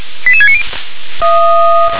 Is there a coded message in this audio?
Yes: five-tone selective call (ZVEI) from 0.3 to 0.6 s, heard as quick runs of five tones, tones of about 70 ms. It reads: 88690